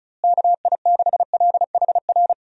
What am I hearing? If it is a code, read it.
Morse code, heard as a keyed tone, about 34 wpm, 700 Hz. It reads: KI6LHR
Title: USA Call Signs - 6 Characters - Repeated 3x - 35wpm